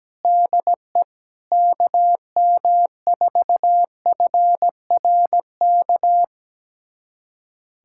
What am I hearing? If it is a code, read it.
Morse code, heard as a keyed tone, about 17 wpm, 700 Hz. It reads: DE KM4FRK
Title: USA Callsign DE Callsign 17wpm